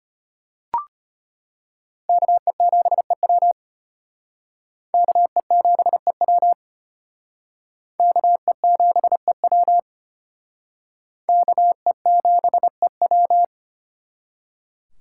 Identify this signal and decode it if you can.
Morse code, heard as a keyed tone, about 34 wpm, 700 Hz. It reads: KE7EW KE7EW KE7EW KE7EW